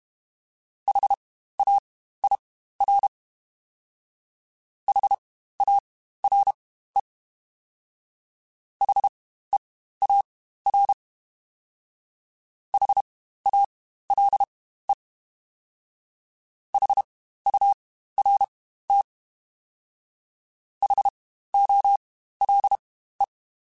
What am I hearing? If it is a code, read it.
Morse code, heard as a keyed tone, about 32 wpm, spaced as for 15 wpm, 800 Hz. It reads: HAIR HARE HEAR HALE HURT HOLE
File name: Spaced Alphas / Spaced Alphas mp3s / Hs sp @30